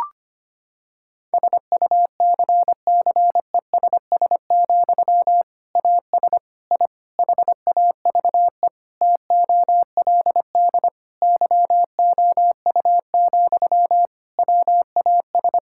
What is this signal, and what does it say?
Morse code, heard as a keyed tone, about 25 wpm, 700 Hz. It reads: SUCCESS, AS I HAVE TOLD YOU, WAS
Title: Sentences from Top 1000 Words - Review Part 6 - 25wpm